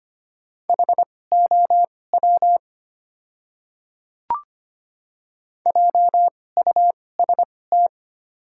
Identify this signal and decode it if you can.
Morse code, heard as a keyed tone, about 25 wpm, spaced as for 18 wpm, 700 Hz. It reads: HOW JUST
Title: 2x Character Spacing - Top 300 Words - 25wpm